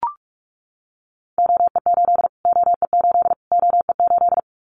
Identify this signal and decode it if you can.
Morse code, heard as a keyed tone, about 44 wpm, 700 Hz. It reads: OE8 OE8 OE8
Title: Callsign Prefixes for Top 25 DXCC - Repeated 3x - 45wpm